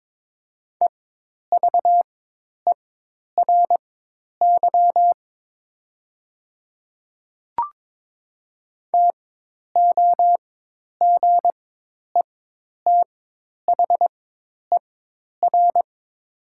Morse code, heard as a keyed tone, about 22 wpm, spaced as for 10 wpm, 700 Hz. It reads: EVERY TOGETHER
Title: 4x Character Spacing - Top 300 Words - 22wpm